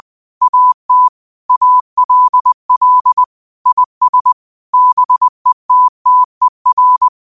Morse code, heard as a keyed tone, about 20 wpm, 1000 Hz. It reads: AT ALL IS BETTER